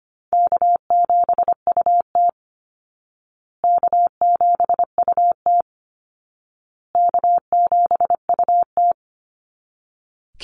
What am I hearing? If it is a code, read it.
Morse code, heard as a keyed tone, about 25 wpm, 700 Hz. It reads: K7UT K7UT K7UT
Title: Easy Call Signs Repeated 3x - 25wpm